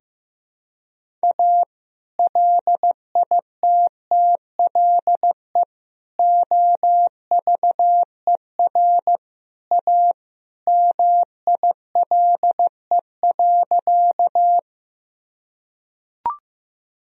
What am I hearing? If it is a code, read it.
Morse code, heard as a keyed tone, about 16 wpm, 700 Hz. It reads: A LITTLE OVER A MILE.
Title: Sentences from Top 1000 Words - Review Part 1 - 15wpm